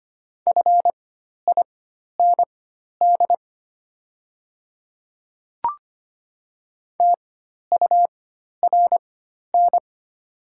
Morse code, heard as a keyed tone, about 25 wpm, spaced as for 12 wpm, 700 Hz. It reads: FIND TURN
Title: 4x Character Spacing - Top 300 Words - 25wpm